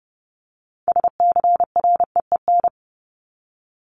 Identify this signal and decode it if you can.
Morse code, heard as a keyed tone, about 30 wpm, 700 Hz. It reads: SCREEN
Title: Top 751-1000 Words 30wpm